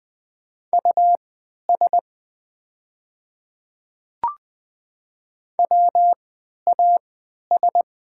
Morse code, heard as a keyed tone, about 20 wpm, spaced as for 11 wpm, 700 Hz. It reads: US WAS
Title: 3x Character Spacing - Top 300 Words - 20wpm